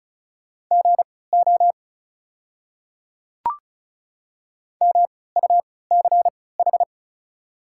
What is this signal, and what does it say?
Morse code, heard as a keyed tone, about 36 wpm, spaced as for 20 wpm, 700 Hz. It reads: GO MUCH
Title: 3x Character Spacing - Top 200 Words - 35wpm